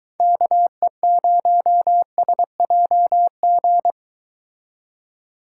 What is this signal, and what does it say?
Morse code, heard as a keyed tone, about 23 wpm, 700 Hz. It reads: KE0SJG